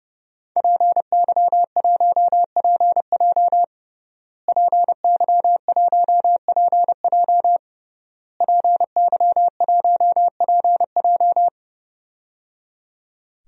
Morse code, heard as a keyed tone, about 29 wpm, 700 Hz. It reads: PY1PJ PY1PJ PY1PJ
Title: DXCC-100 Call Signs - 5 Characters - Repeated 3x - 30wpm